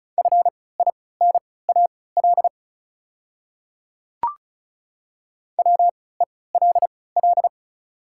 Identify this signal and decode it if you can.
Morse code, heard as a keyed tone, about 36 wpm, spaced as for 20 wpm, 690 Hz. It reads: FINAL WELL
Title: 3x Character Spacing - Top 500 Words - 35wpm